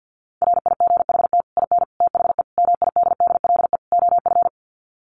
Morse code, heard as a keyed tone, about 50 wpm, 700 Hz. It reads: RIGHT IN THE MIDDLE OF
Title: Sets of 5 Words 50wpm